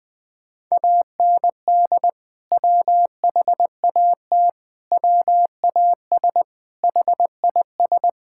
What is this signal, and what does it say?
Morse code, heard as a keyed tone, about 20 wpm, 700 Hz. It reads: AND WHAT WAS HIS
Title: Sentences from Top 1600 Words 20wpm